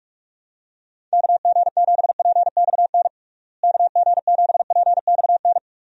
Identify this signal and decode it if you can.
Morse code, heard as a keyed tone, about 46 wpm, 700 Hz. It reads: KG7PXN KG7PXN
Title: USA Call Signs - 6 Characters - Repeated 2x - 45wpm